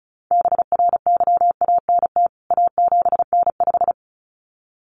Morse code, heard as a keyed tone, about 34 wpm, 700 Hz. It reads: BRYANT A7N5
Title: CWT Exchange 35wpm